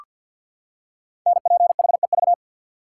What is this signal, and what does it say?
Morse code, heard as a keyed tone, about 48 wpm, 700 Hz. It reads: NPHEV
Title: Code Groups - 5 Letters - 50wpm